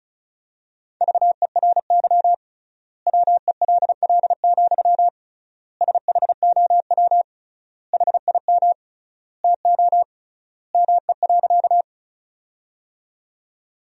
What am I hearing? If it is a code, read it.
Morse code, heard as a keyed tone, about 36 wpm, 700 Hz. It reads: VERY WELL, SHOW HIM TO ME.
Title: 3x Word-Spacing - Sentences from Top 200 Words - 35wpm